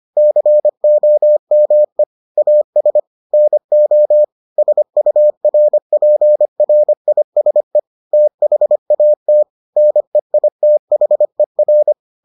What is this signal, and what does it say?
Morse code, heard as a keyed tone, about 25 wpm, 600 Hz. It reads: COME AS NO SURPRISE THAT NEITHER